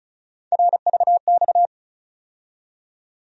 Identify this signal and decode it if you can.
Morse code, heard as a keyed tone, about 36 wpm, 700 Hz. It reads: RVX
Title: USA Call Sign Postfixes - 3 Letters 35wpm